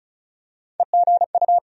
Morse code, heard as a keyed tone, about 34 wpm, 700 Hz. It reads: EGU
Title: USA Call Sign Postfixes - 3 Letters 35wpm